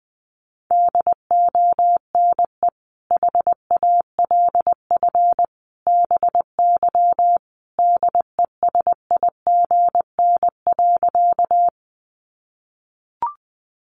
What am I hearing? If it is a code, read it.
Morse code, heard as a keyed tone, about 20 wpm, 700 Hz. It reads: DONE HALF BY DESIGN.
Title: Sentences from Top 800 Words 20wpm